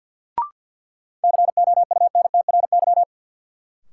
Morse code, heard as a keyed tone, about 53 wpm, 700 Hz. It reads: COUNTRY
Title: Sentences from Top 300 Words - New Word Drill - 50wpm